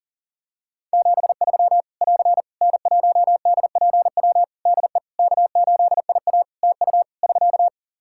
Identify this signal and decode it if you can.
Morse code, heard as a keyed tone, about 37 wpm, 700 Hz. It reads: 73 <AR> N1DPW DE K8IA TU <SK>
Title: K1USN SST QSO Practice 40wpm